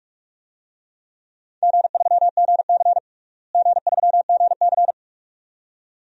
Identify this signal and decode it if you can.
Morse code, heard as a keyed tone, about 46 wpm, 700 Hz. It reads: G3GC G3GC
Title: DXCC-100 Call Signs - 4 Characters - Repeated 2x - 45wpm